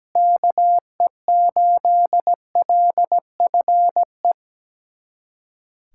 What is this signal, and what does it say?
Morse code, heard as a keyed tone, about 17 wpm, 700 Hz. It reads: KE8LFE
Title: USA Callsign DE Callsign - 6 Characters - 17wpm